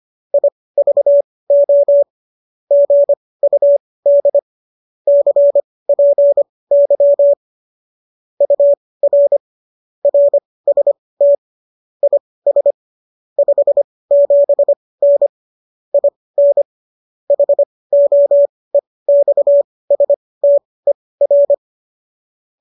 Morse code, heard as a keyed tone, about 25 wpm, spaced as for 18 wpm, 570 Hz. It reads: IVO GUD CPY UR RST IS 57N IN HOEXSTER